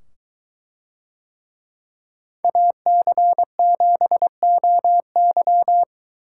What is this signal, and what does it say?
Morse code, heard as a keyed tone, about 21 wpm, 700 Hz. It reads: AC7OY